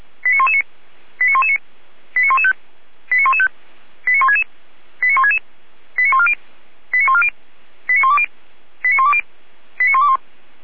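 Five-tone selective call (ZVEI), heard as quick runs of five tones, tones of about 70 ms. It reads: 89119 89119 89115 89115 89166 89166 89140 89140 89120 89120 89121
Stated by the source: ZVEI five-tone selective call received on an SDR, logged by Feuerwehr Orlamünde (f85360).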